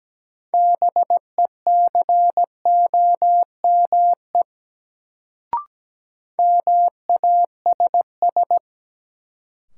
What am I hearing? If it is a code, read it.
Morse code, heard as a keyed tone, about 17 wpm, 700 Hz. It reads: BECOME MASS
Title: Sets of 5 Words Encore - New Word Drill - 17wpm